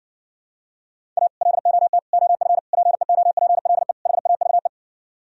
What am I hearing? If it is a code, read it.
Morse code, heard as a keyed tone, about 70 wpm, 700 Hz. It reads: A LOT OF PEOPLE HAVE